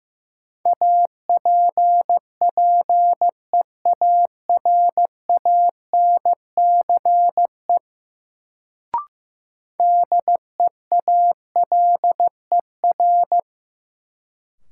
Morse code, heard as a keyed tone, about 15 wpm, 700 Hz. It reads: APPEARANCE DEALER
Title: Sentences from Top 1100 Words - New Word Drill - 15wpm